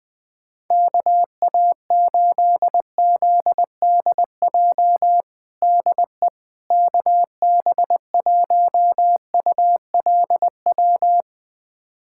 Morse code, heard as a keyed tone, about 20 wpm, 700 Hz. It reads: KA8ZDJ DE KB1ULW
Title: USA Callsign DE Callsign 20wpm